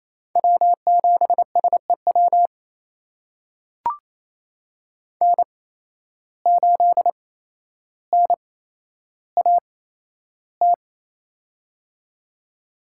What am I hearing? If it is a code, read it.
Morse code, heard as a keyed tone, about 28 wpm, spaced as for 12 wpm, 700 Hz. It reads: W7SEW N8NAT
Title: USA Call Signs - 5 Characters - 8x Character Spacing - 28wpm